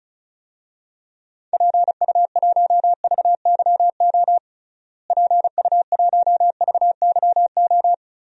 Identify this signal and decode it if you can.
Morse code, heard as a keyed tone, about 34 wpm, 700 Hz. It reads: PU1VYO PU1VYO
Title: DXCC-100 Call Signs - 6 Characters - Repeated 2x - 35wpm